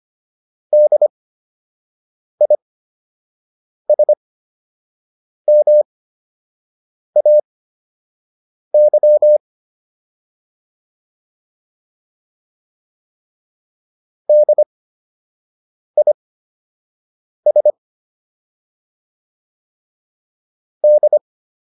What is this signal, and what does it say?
Morse code, heard as a keyed tone, about 25 wpm, spaced as for 6 wpm, 600 Hz. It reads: DISMAY DIS D